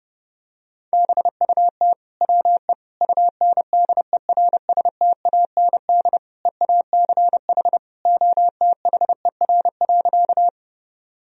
Morse code, heard as a keyed tone, about 32 wpm, 700 Hz. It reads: BUT WE UNDERSTAND EACH OTHER.
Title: Sentences from Top 1000 Words - Review Part 1 - 30wpm